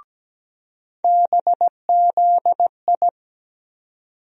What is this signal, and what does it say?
Morse code, heard as a keyed tone, about 17 wpm, 700 Hz. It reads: BZI
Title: USA Call Sign Postfixes - 3 Letters 17wpm